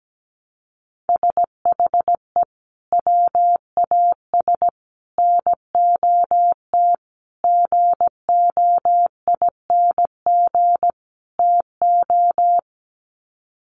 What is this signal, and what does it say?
Morse code, heard as a keyed tone, about 17 wpm, 700 Hz. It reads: SHE WAS NOT GOING TO
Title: Sets of 5 Words Encore2 17wpm